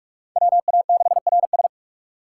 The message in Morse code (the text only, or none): WA6RS